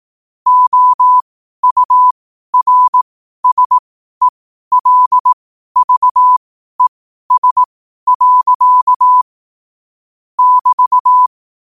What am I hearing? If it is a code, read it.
Morse code, heard as a keyed tone, about 18 wpm, spaced as for 12 wpm, 1000 Hz. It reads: OURSELVES. <BT>
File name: qotdfiles / 12wpm